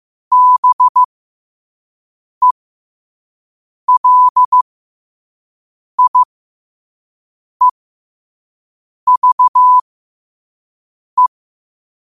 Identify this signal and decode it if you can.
Morse code, heard as a keyed tone, about 15 wpm, spaced as for 5 wpm, 1000 Hz. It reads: BELIEVE
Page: New quotes every day in morse code at 5 Words per minute.